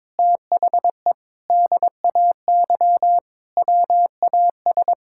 Morse code, heard as a keyed tone, about 22 wpm, 700 Hz. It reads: THE DAY WAS
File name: practice / 22wpm / Sentences with 4-Word Phrases - 22wpm